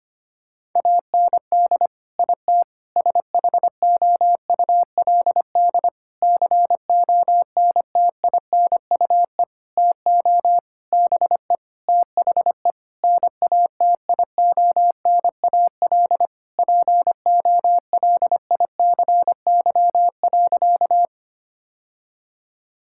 Morse code, heard as a keyed tone, about 25 wpm, 700 Hz. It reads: AND IT SHOULD CONTINUE TO BE THE NATIONAL POLICY.
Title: Sentences from Top 1000 Words - Review Part 6 - 25wpm